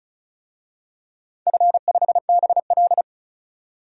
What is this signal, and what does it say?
Morse code, heard as a keyed tone, about 36 wpm, 700 Hz. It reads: F5BL
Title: DXCC-100 Call Signs - 4 Characters - 35wpm